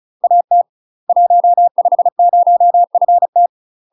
Morse code, heard as a keyed tone, about 36 wpm, 700 Hz. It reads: AT 150FT
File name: UK MASTER QSOs TRAINING / 35WPM / UK MASTER QSOs 0017